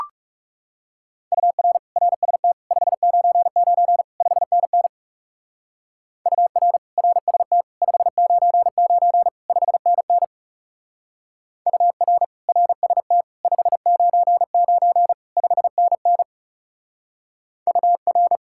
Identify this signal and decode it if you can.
Morse code, heard as a keyed tone, about 37 wpm, 700 Hz. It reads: UR RST 599 5NN UR RST 599 5NN UR RST 599 5NN UR
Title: Speed-Racing Common QSO Phrases Encore - 30wpm